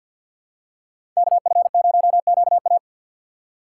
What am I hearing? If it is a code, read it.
Morse code, heard as a keyed tone, about 53 wpm, 700 Hz. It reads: KF0QA
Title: USA Call Signs - 5 Characters - 3x Character Spacing - 50wpm